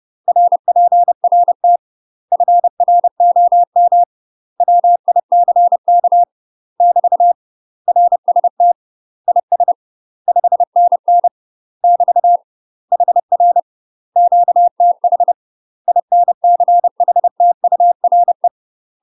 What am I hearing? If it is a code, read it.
Morse code, heard as a keyed tone, about 30 wpm, 700 Hz. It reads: RPRT FROM WICK <BT> RST IS 5NN <BT> HR QTH INCHTURE
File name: UK MASTER QSOs TRAINING / 30WPM / UK MASTER QSOs 0005